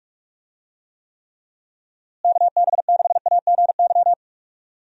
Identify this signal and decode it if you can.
Morse code, heard as a keyed tone, about 46 wpm, 700 Hz. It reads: KB6AGY
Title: USA Call Signs - 6 Characters - Repeated 3x - 45wpm